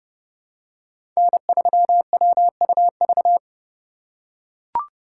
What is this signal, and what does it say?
Morse code, heard as a keyed tone, about 32 wpm, 700 Hz. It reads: N3WUV
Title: USA Call Signs - 5 Characters - 3x Character Spacing - 30wpm